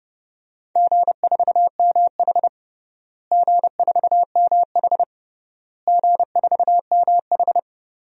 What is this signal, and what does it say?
Morse code, heard as a keyed tone, about 30 wpm, 700 Hz. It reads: G4MH G4MH G4MH
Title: DXCC-100 Call Signs - 4 Characters - Repeated 3x - 30wpm